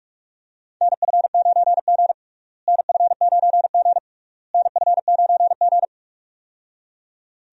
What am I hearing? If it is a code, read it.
Morse code, heard as a keyed tone, about 46 wpm, 700 Hz. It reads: NF9G NF9G NF9G
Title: USA Call Signs - 4 Characters - Repeated 3x - 45wpm